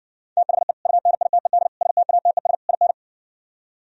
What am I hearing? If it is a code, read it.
Morse code, heard as a keyed tone, about 60 wpm, 700 Hz. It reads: THE UNITED STATES IN